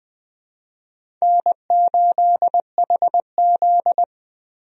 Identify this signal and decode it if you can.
Morse code, heard as a keyed tone, about 20 wpm, 700 Hz. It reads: N8HZ
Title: USA Call Signs - 4 Characters - Repeated 3x - 20wpm